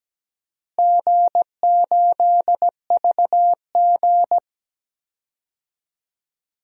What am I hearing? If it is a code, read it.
Morse code, heard as a keyed tone, about 17 wpm, 700 Hz. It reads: G8VG
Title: DXCC-100 Call Signs - 4 Characters - Repeated 3x - 17wpm